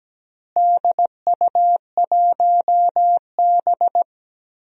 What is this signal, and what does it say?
Morse code, heard as a keyed tone, about 17 wpm, 700 Hz. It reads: DU1B
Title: DXCC-100 Call Signs - 4 Characters - Repeated 3x - 17wpm